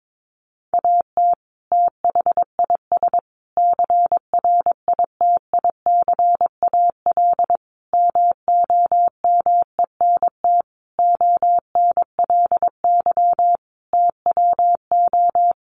Morse code, heard as a keyed tone, about 22 wpm, 700 Hz. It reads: AT THIS CRITICAL MOMENT ONLY TWO
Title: Sentences from Top 1100 Words 22wpm